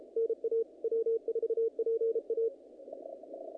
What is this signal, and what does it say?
Morse code, heard as a keyed tone, about 32 wpm, 460 Hz, with radio noise. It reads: NA W4PA
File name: contests / naqp-cw-k5tr-op-k5pi- -2000